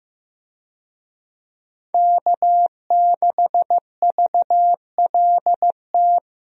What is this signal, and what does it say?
Morse code, heard as a keyed tone, about 15 wpm, 700 Hz. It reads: K6VLT
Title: USA Call Signs - 5 Characters - Repeated 3x - 15wpm